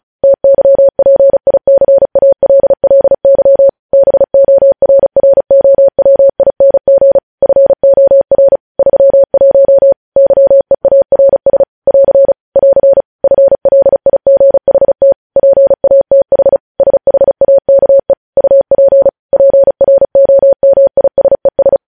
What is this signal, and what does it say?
Morse code, heard as a keyed tone, about 36 wpm, 550 Hz. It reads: TYPICALLY BORROWING FOR 31 YEARS <AR> <AR> FLIGHT PATH SHAKE UP PROMISES
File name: podcast / News headlines-1x-35x35